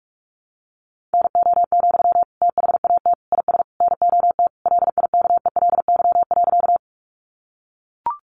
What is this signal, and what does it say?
Morse code, heard as a keyed tone, about 44 wpm, 700 Hz. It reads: NO, THAT IS NOT LIKELY.
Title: Sentences from Top 600 Words 45wpm